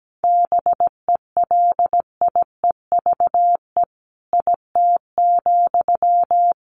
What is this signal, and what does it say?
Morse code, heard as a keyed tone, about 17 wpm, 700 Hz. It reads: BELIEVE IT,
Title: Sentences from Top 1000 Words 17wpm